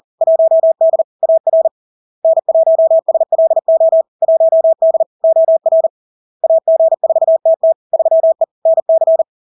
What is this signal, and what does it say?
Morse code, heard as a keyed tone, about 40 wpm, 650 Hz. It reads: 1D AR N1SLO 1D OR AG4TT 3E NC
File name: LARK-FD-2022-Log-in-CW-40wpm-650-Hz